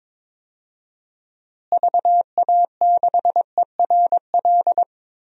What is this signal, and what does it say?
Morse code, heard as a keyed tone, about 22 wpm, 700 Hz. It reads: VA6ERL
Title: DXCC-100 Call Signs - 6 Characters - Repeated 2x - 22wpm